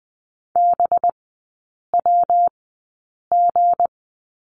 Morse code, heard as a keyed tone, about 20 wpm, spaced as for 8 wpm, 700 Hz. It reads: BWG
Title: 191 Lesson 31e - Letter v - Three Characters 20wpm